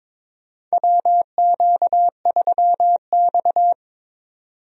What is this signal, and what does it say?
Morse code, heard as a keyed tone, about 22 wpm, 700 Hz. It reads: WQ3X
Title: USA Call Signs - 4 Characters - 22wpm